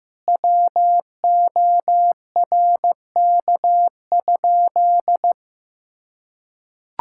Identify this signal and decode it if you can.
Morse code, heard as a keyed tone, about 15 wpm, 700 Hz. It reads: WORK?
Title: Sentences from Top 1000 Words - Review Part 5 - 15wpm